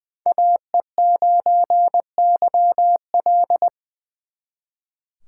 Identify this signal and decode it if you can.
Morse code, heard as a keyed tone, about 23 wpm, 700 Hz. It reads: AE9YL
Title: Speed Racing - USA 2x2 Call Signs 20wpm